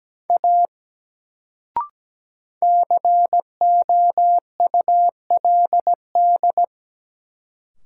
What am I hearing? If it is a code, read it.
Morse code, heard as a keyed tone, about 17 wpm, 700 Hz. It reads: A COULD
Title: Sentences from Top 100 Words - New Word Drill - 17wpm